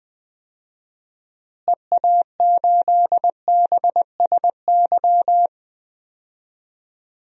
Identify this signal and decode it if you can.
Morse code, heard as a keyed tone, about 20 wpm, 700 Hz. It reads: EA8BSY
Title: DXCC-100 Call Signs - 6 Characters - Repeated 2x - 20wpm